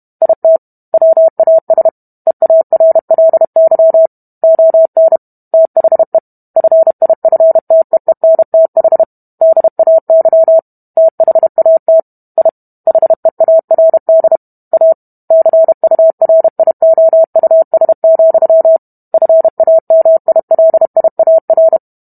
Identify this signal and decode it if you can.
Morse code, heard as a keyed tone, about 32 wpm, 650 Hz. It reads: IT WAS EARLY ON THE FIFTEENTH DAY THAT I HEARD A CURIOUS, FAMILIAR